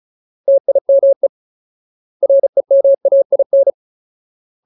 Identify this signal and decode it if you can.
Morse code, heard as a keyed tone, about 34 wpm, 550 Hz. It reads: TIME REMAIN
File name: Ultimate Word List / 35WPM Ultimate